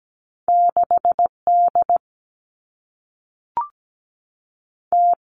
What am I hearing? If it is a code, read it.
Morse code, heard as a keyed tone, about 17 wpm, 700 Hz. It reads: 6 D T
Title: Two Challenging Letter-Number Combinations 17wpm